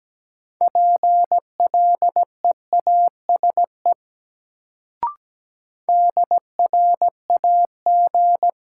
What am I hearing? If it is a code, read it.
Morse code, heard as a keyed tone, about 17 wpm, 700 Hz. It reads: PLEASE DRAG
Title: Sentences from Top 1500 Words - New Word Drill - 17wpm